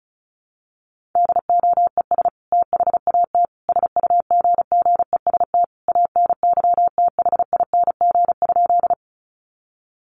Morse code, heard as a keyed tone, about 34 wpm, 700 Hz. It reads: DOES THAT SUGGEST ANYTHING?
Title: Sentences from Top 1300 Words 35wpm